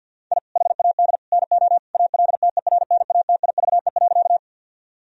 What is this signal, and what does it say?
Morse code, heard as a keyed tone, about 50 wpm, 700 Hz. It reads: I HAD NO ALTERNATIVE.
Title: Sentences from Top 1000 Words - Review Part 4 - 50wpm